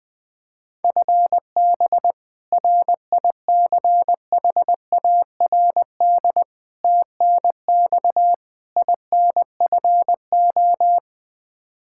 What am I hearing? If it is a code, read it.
Morse code, heard as a keyed tone, about 21 wpm, 700 Hz. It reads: FB RICHARD TNX INFO
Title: Practice QSO - Exchange 3 - More Info - 20wpm